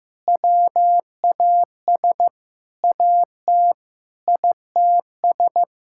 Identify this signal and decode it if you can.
Morse code, heard as a keyed tone, about 15 wpm, 700 Hz. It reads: WAS AT ITS
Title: Sentences from Top 1600 Words 15wpm